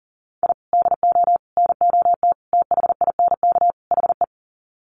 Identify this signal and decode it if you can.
Morse code, heard as a keyed tone, about 37 wpm, 700 Hz. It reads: I DO NOT THINK HE
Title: Sets of 5 Words 40wpm